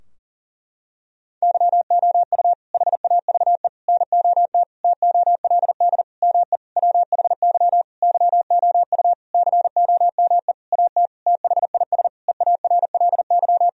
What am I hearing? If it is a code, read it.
Morse code, heard as a keyed tone, about 40 wpm, 700 Hz. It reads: YOU HAVE NOT TOLD ME WHY YOU COME AT THIS EARLY
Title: Sentences from Top 1000 Words - Review Part 2 - 40wpm